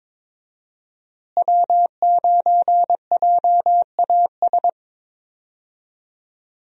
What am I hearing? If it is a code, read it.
Morse code, heard as a keyed tone, about 22 wpm, 700 Hz. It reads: W9JAS
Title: USA Call Signs - 5 Characters - Repeated 3x - 22wpm